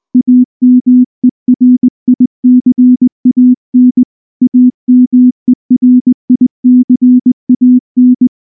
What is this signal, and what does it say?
Morse code, heard as a keyed tone, about 19 wpm, 250 Hz. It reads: AMERICAN AMERICAN
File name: morse / morse-training-20WPM-10